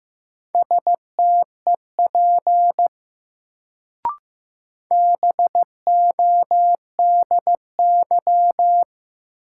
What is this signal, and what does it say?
Morse code, heard as a keyed tone, about 15 wpm, 700 Hz. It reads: STEP BODY
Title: Sentences from Top 400 Words - New Word Drill - 15wpm